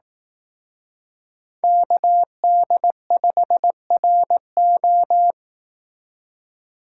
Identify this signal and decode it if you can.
Morse code, heard as a keyed tone, about 18 wpm, 700 Hz. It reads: KD5RO
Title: Speed Racing - USA 2x2 Call Signs 15wpm